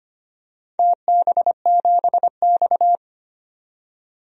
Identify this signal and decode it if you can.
Morse code, heard as a keyed tone, about 25 wpm, 700 Hz. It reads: TB7X
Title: DXCC-100 Call Signs - 4 Characters - 25wpm